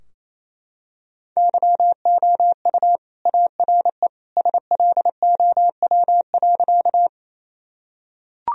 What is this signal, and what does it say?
Morse code, heard as a keyed tone, about 28 wpm, 700 Hz. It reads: YOU ARE SLOW.
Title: Sentences from Top 1000 Words - Review Part 2 - 28wpm